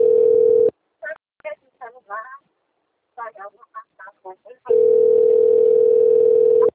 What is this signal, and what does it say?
Ringback tone (North America), 440 + 480 Hz.